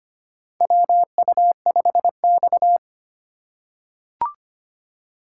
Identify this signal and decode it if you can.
Morse code, heard as a keyed tone, about 25 wpm, 700 Hz. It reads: WU5X
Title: USA Call Signs - 4 Characters - 4x Character Spacing - 25wpm